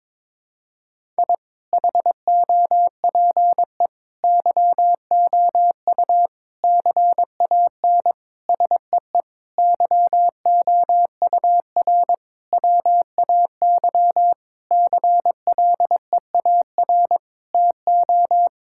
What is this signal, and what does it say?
Morse code, heard as a keyed tone, about 21 wpm, 700 Hz. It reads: I HOPE YOU CAN SEE YOUR WAY CLEAR TO
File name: Sets of 9 Words - 22wpm